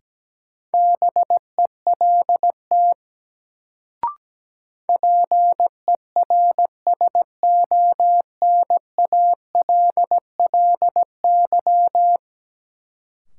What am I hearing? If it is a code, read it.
Morse code, heard as a keyed tone, about 17 wpm, 700 Hz. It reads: BELT PERSONALLY
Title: Sentences from Top 1400 Words - New Word Drill - 17wpm